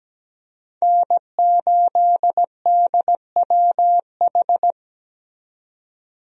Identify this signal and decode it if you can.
Morse code, heard as a keyed tone, about 17 wpm, 700 Hz. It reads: N8DWH
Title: USA Call Signs - 5 Characters - 17wpm